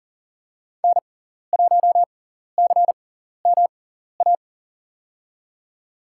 Morse code, heard as a keyed tone, about 40 wpm, spaced as for 14 wpm, 700 Hz. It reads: N1CMA
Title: USA Call Signs - 6x Character Spacing - 40wpm